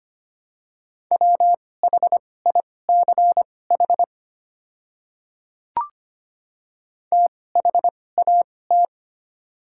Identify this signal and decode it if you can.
Morse code, heard as a keyed tone, about 25 wpm, spaced as for 18 wpm, 700 Hz. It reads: WHICH THAT
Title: 2x Character Spacing - Top 200 Words - 25wpm